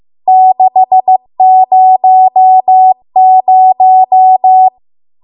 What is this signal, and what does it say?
Morse code, heard as a keyed tone, about 15 wpm, 750 Hz. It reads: 600
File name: Archive / 15 WPM / 15WPM